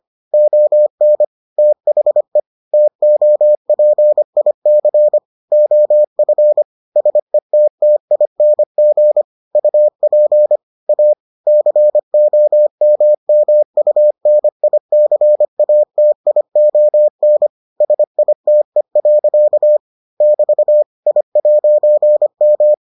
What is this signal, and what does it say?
Morse code, heard as a keyed tone, about 25 wpm, 600 Hz. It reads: ON THE TOPIC OF SETTING UP A COMMUNICATION SITE. <BT> I'M